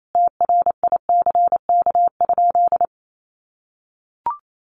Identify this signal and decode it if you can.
Morse code, heard as a keyed tone, about 28 wpm, 700 Hz. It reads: TRICK?
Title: Sentences from Top 1500 Words 28wpm